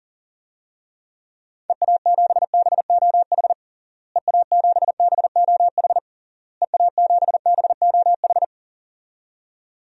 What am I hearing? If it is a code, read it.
Morse code, heard as a keyed tone, about 40 wpm, 700 Hz. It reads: EA7BOH EA7BOH EA7BOH
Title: DXCC-100 Call Signs - 6 Characters - Repeated 3x - 40wpm